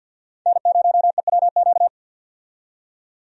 Morse code, heard as a keyed tone, about 53 wpm, 700 Hz. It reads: N0EWQ
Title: USA Call Signs - 5 Characters - 50wpm